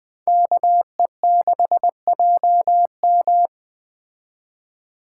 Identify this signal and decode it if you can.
Morse code, heard as a keyed tone, about 20 wpm, 700 Hz. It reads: KE6JM
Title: Speed Racing - USA Call Signs 17wpm